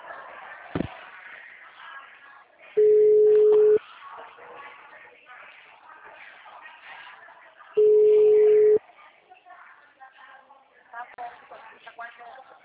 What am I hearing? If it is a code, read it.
Ringback tone (Europe, 425 Hz), with speech.